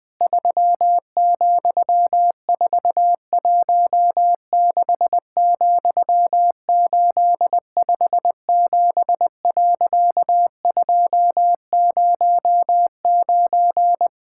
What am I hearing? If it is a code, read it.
Morse code, heard as a keyed tone, about 20 wpm, 700 Hz. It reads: 3,416,857.209
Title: Pangram Challenge v2 - 20wpm